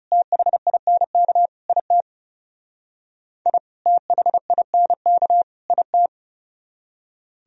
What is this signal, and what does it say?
Morse code, heard as a keyed tone, about 38 wpm, 700 Hz. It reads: THINK IT I THINK IT